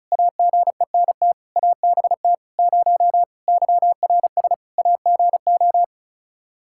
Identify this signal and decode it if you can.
Morse code, heard as a keyed tone, about 36 wpm, 700 Hz. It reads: AGENT ABT 0 YRS AGO
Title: Practice QSO - Exchange 3 - More Info - 35wpm